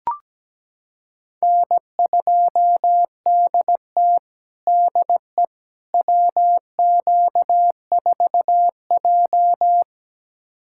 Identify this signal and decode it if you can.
Morse code, heard as a keyed tone, about 17 wpm, 700 Hz. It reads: N2DT DE WQ4J
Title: USA Callsign DE Callsign - 4 Characters - 17wpm